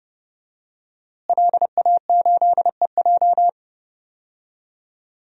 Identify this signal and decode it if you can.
Morse code, heard as a keyed tone, about 30 wpm, 700 Hz. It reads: LA8EJ
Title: DXCC-100 Call Signs - 5 Characters - Repeated 2x - 30wpm